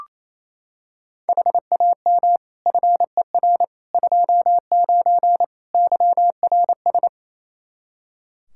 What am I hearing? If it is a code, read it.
Morse code, heard as a keyed tone, about 27 wpm, 700 Hz. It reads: HAM FER 29 YRS
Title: Practice QSO - Exchange 3 - More Info - 28wpm